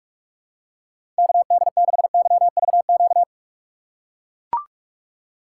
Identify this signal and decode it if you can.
Morse code, heard as a keyed tone, about 44 wpm, 700 Hz. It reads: KD6YVQ
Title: USA Call Signs - 6 Characters - 6x Character Spacing - 45wpm